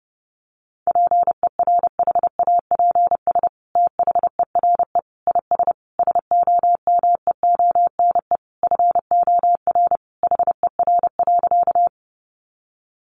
Morse code, heard as a keyed tone, about 30 wpm, 700 Hz. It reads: PERHAPS THERE IS SOMEONE FOR HER.
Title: Sentences from Top 600 Words 30wpm